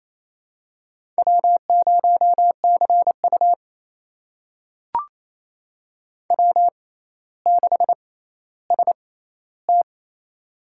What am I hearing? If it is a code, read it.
Morse code, heard as a keyed tone, about 28 wpm, spaced as for 14 wpm, 700 Hz. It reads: W0CU W6ST
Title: USA Call Signs - 4 Characters - 6x Character Spacing - 28wpm